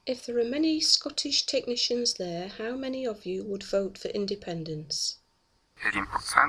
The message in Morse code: none